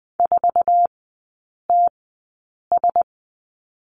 Morse code, heard as a keyed tone, about 20 wpm, spaced as for 8 wpm, 700 Hz. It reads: 4TS
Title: 067 Lesson 12e - Letter d - Three Characters 20wpm